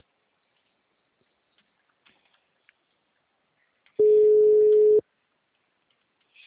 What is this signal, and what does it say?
Ringback tone (Europe, 425 Hz).